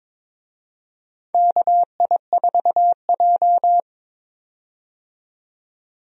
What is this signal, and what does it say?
Morse code, heard as a keyed tone, about 22 wpm, 700 Hz. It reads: KI4J
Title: USA Call Signs - 4 Characters - 22wpm